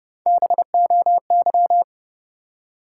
Morse code, heard as a keyed tone, about 30 wpm, 700 Hz. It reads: BOY
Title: Three Letter Words - 30wpm